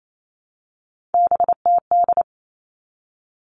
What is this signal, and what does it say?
Morse code, heard as a keyed tone, about 28 wpm, 700 Hz. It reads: BTD
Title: All Cut Numbers - 3-Digit 28wpm